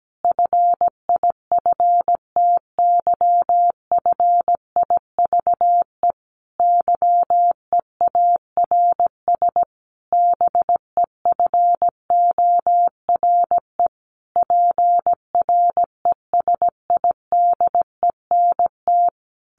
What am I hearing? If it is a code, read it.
Morse code, heard as a keyed tone, about 17 wpm, 700 Hz. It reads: FIFTYFIVE YEARS BEFORE PRESIDENT
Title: A Sound Of Thunder 17wpm